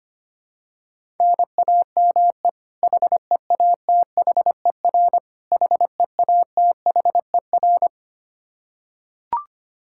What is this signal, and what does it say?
Morse code, heard as a keyed tone, about 25 wpm, 700 Hz. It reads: NAME HEATHER HEATHER
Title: K1USN SST QSO Practice 25wpm